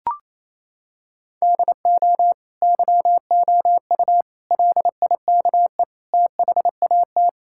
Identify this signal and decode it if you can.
Morse code, heard as a keyed tone, about 28 wpm, 700 Hz. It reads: DO YOU LIKE THAT
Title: Sentences from Top 1000 Words - Review Part 5 - 28wpm